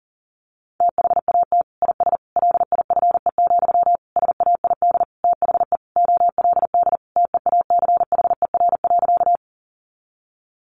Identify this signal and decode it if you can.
Morse code, heard as a keyed tone, about 37 wpm, 700 Hz. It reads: THAT IS LIFE, SAID THE OLD TEACHER.
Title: Sentences from Top 900 Words 40wpm